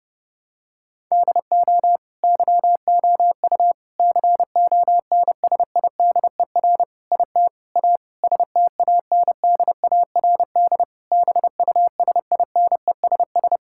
Morse code, heard as a keyed tone, about 30 wpm, 700 Hz. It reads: DO YOU CONSIDER IT A STANDARD BUSINESS
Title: Sentences from Top 1000 Words - Review Part 6 - 30wpm